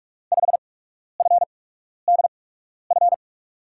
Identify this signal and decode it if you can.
Morse code, heard as a keyed tone, about 43 wpm, 700 Hz. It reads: 5 F D F
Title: USA Call Signs - 8x Character Spacing - 45wpm